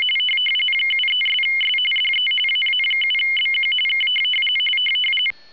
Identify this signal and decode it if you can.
RTTY, about 45.45 baud. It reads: 73'S DE AA7BQ K
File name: w ry850 / rtty45